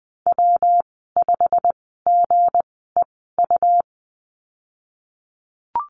Morse code, heard as a keyed tone, about 20 wpm, spaced as for 14 wpm, 700 Hz. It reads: W5GEU
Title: 161 Lesson 26g - Letter g - Call Signs 20wpm